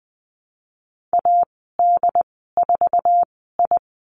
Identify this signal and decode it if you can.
Morse code, heard as a keyed tone, about 20 wpm, spaced as for 14 wpm, 700 Hz. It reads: AD4I
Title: 140 Lesson 23g - Letter f - Call Signs 20wpm